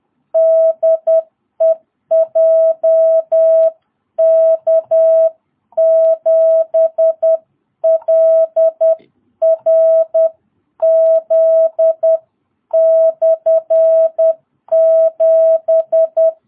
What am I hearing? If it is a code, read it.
Morse code, heard as a keyed tone, about 10 wpm, 660 Hz. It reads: DE JK7LRZ/7
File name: jh8cbh / 2022cw